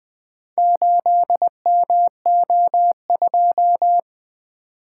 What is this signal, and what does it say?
Morse code, heard as a keyed tone, about 20 wpm, 700 Hz. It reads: 8MO2
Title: Four Letter-Number Combinations - 20wpm